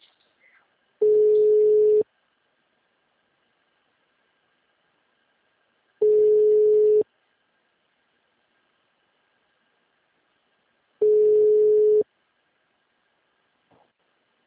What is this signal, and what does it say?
Ringback tone (Europe, 425 Hz).